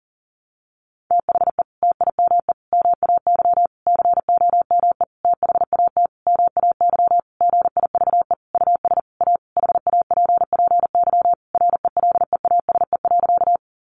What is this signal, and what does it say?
Morse code, heard as a keyed tone, about 40 wpm, 700 Hz. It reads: THE TIME MAY COME THAT MAY GIVE US A HAPPY RELEASE.
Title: Sentences from Top 1100 Words 40wpm